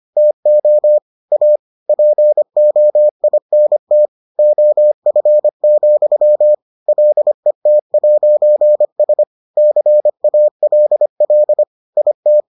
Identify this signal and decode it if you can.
Morse code, heard as a keyed tone, about 25 wpm, 600 Hz. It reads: TO A POINT OF, LET'S CALL IT